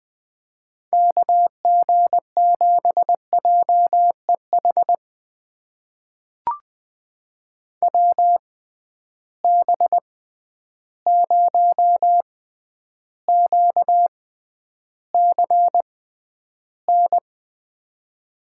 Morse code, heard as a keyed tone, about 20 wpm, spaced as for 10 wpm, 700 Hz. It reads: KG7JEH WB0QCN
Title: USA Call Signs - 6 Characters - 6x Character Spacing - 20wpm